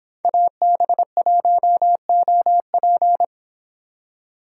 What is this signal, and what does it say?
Morse code, heard as a keyed tone, about 27 wpm, 700 Hz. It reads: AB1OP
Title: Speed Racing - USA 2x2 Call Signs 22wpm